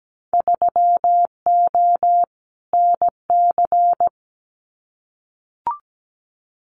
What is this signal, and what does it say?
Morse code, heard as a keyed tone, about 17 wpm, 700 Hz. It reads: 3O NC
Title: Winter Field Day Exchanges 17wpm